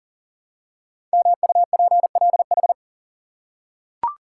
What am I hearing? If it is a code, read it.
Morse code, heard as a keyed tone, about 39 wpm, 700 Hz. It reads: MUPLH
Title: Code Groups - 5 Letters - 40wpm